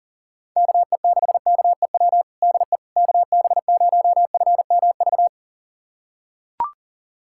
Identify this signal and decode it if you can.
Morse code, heard as a keyed tone, about 40 wpm, 700 Hz. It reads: KE6KEW DE KB0FMV
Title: USA Callsign DE Callsign 40wpm